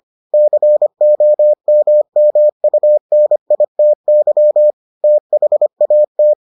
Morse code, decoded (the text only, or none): COMMUNITY THAT